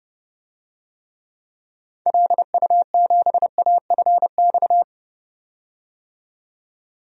Morse code, heard as a keyed tone, about 30 wpm, 700 Hz. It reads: LU7AFX
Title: DXCC-100 Call Signs - 6 Characters - 30wpm